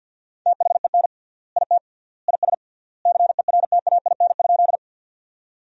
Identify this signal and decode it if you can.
Morse code, heard as a keyed tone, about 50 wpm, 700 Hz. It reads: THEN IT IS CERTAIN?
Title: 3x Word-Spacing - Sentences from Top 500 Words - 50wpm